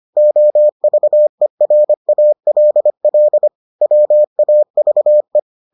Morse code, heard as a keyed tone, about 25 wpm, 600 Hz. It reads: OVERALL WAVE